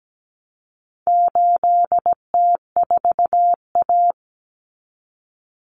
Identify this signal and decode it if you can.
Morse code, heard as a keyed tone, about 17 wpm, 700 Hz. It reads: 8T4A
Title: Common Cut Numbers - 4-Digit 17wpm